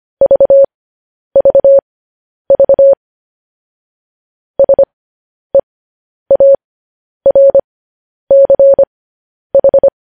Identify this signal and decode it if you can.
Morse code, heard as a keyed tone, about 25 wpm, spaced as for 10 wpm, 550 Hz. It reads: VVV SEARCH
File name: podcast / News headlines-1x-25x10